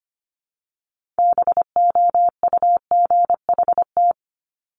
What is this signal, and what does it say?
Morse code, heard as a keyed tone, about 25 wpm, 700 Hz. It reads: BOUGHT
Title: Multi-letter Phonemes Repeated 3x 25wpm